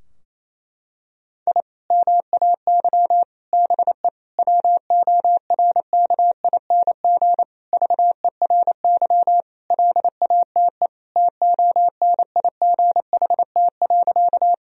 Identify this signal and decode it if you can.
Morse code, heard as a keyed tone, about 28 wpm, 700 Hz. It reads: I MAY BE WORKING VERY LATE TONIGHT.
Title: Sentences from Top 1000 Words - Review Part 4 - 28wpm